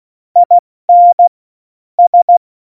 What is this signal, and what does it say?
Morse code, heard as a keyed tone, about 16 wpm, 700 Hz. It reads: IN S
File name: Morsekurs / Witze / Witz 14. August 2018, 70 F80